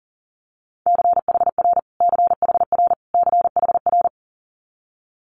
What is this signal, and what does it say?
Morse code, heard as a keyed tone, about 40 wpm, 700 Hz. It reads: CHR CHR CHR
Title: Multi-letter Phonemes Repeated 3x 40wpm